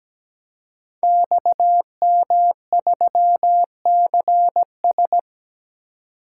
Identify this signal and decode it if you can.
Morse code, heard as a keyed tone, about 17 wpm, 700 Hz. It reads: XM3CS
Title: Code Groups - 5 Letter:Numbers - 17wpm